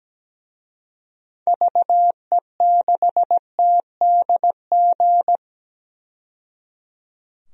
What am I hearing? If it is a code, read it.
Morse code, heard as a keyed tone, about 17 wpm, 700 Hz. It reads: VE6TDG
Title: DXCC-100 Call Signs - 6 Characters - 17wpm